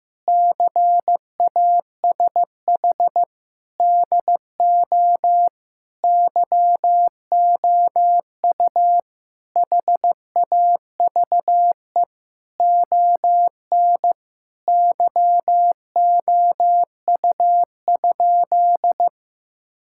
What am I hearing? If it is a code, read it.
Morse code, heard as a keyed tone, about 15 wpm, 700 Hz. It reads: CASH DO YOU HAVE ON YOU?
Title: Sentences from Top 1000 Words - Review Part 5 - 15wpm